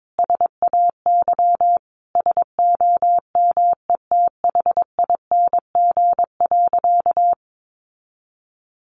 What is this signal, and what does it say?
Morse code, heard as a keyed tone, about 22 wpm, 700 Hz. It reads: SAY SOMETHING.
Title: Sentences from Top 1200 Words 22wpm